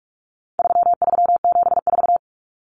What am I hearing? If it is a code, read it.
Morse code, heard as a keyed tone, about 44 wpm, 700 Hz. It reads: 3374
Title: Numbers - 4-Digit 45wpm